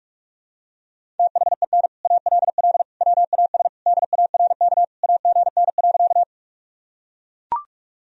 Morse code, heard as a keyed tone, about 41 wpm, 700 Hz. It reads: THEN ALL WAS DARK AGN.
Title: Sentences from Top 500 Words and Abbreviations 45wpm